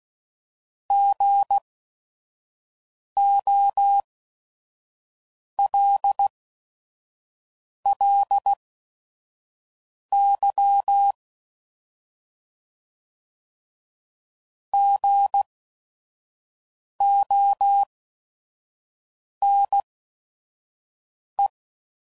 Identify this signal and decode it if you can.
Morse code, heard as a keyed tone, about 16 wpm, spaced as for 5 wpm, 800 Hz. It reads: GOLLY GONE
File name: Prefixes / By First Letter / Gs